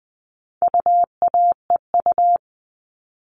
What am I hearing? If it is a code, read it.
Morse code, heard as a keyed tone, about 20 wpm, 700 Hz. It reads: UAEU
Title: All Cut Numbers - 4-Digit 20wpm